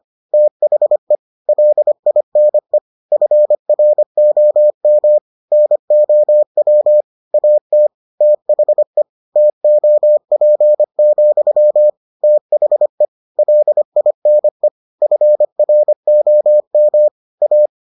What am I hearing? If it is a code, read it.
Morse code, heard as a keyed tone, about 25 wpm, 600 Hz. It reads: THE LINE FROM NOW AT THE TOP, THE LINE FROM A